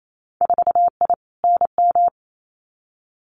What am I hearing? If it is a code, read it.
Morse code, heard as a keyed tone, about 28 wpm, 700 Hz. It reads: 4I NM
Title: Winter Field Day Exchanges Encore 28wpm